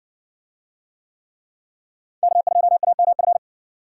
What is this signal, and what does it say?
Morse code, heard as a keyed tone, about 60 wpm, 700 Hz. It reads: K2AMF